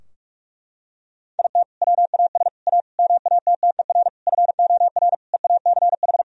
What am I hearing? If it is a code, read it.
Morse code, heard as a keyed tone, about 46 wpm, 700 Hz. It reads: IT WAS A MATTER FOR EACH
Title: Sentences from Top 1000 Words - Review Part 4 - 45wpm